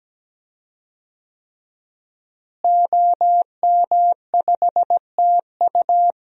Morse code, heard as a keyed tone, about 17 wpm, 700 Hz. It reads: OM5TU
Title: DXCC-100 Call Signs - 5 Characters - Repeated 3x - 17wpm